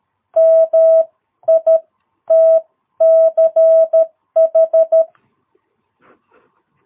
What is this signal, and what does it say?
Morse code, heard as a keyed tone, about 13 wpm, spaced as for 11 wpm, 660 Hz. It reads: MITCH
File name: jh8cbh / 2022cw